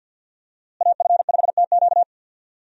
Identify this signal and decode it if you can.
Morse code, heard as a keyed tone, about 51 wpm, 700 Hz. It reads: AF5TQ
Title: USA Call Signs - 5 Characters - Repeated 2x - 50wpm